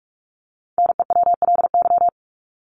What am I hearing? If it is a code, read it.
Morse code, heard as a keyed tone, about 44 wpm, 700 Hz. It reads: NEWLY
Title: LY Suffix - Top 100 Words with Base Words 45wpm